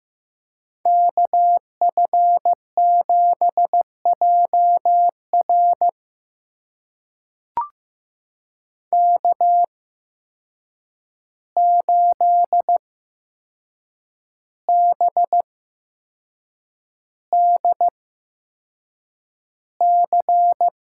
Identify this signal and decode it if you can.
Morse code, heard as a keyed tone, about 15 wpm, spaced as for 6 wpm, 700 Hz. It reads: KF7JR K8BDC
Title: USA Call Signs - 5 Characters - 8x Character Spacing - 15wpm